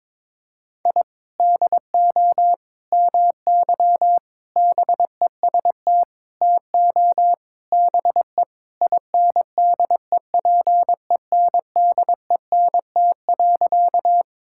Morse code, heard as a keyed tone, about 22 wpm, 700 Hz. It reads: I DO MY BEST TO BE INDEPENDENT.
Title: Sentences from Top 1000 Words - Review Part 6 - 22wpm